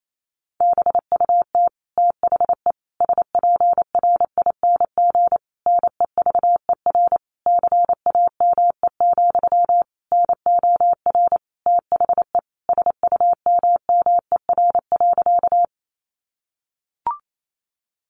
Morse code, heard as a keyed tone, about 28 wpm, 700 Hz. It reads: BUT THE SPRING NEVER CAME, NOR THE SUMMER.
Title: Sentences from Top 1500 Words 28wpm